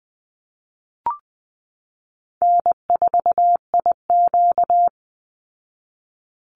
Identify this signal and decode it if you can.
Morse code, heard as a keyed tone, about 20 wpm, 700 Hz. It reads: N4IQ
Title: Easy Call Signs Repeated 2x - 20wpm